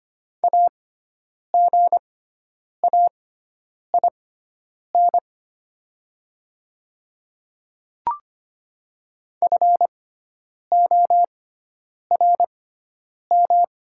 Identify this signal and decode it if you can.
Morse code, heard as a keyed tone, about 25 wpm, spaced as for 9 wpm, 700 Hz. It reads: AGAIN FORM
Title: 6x Character Spacing - Top 200 Words - 25wpm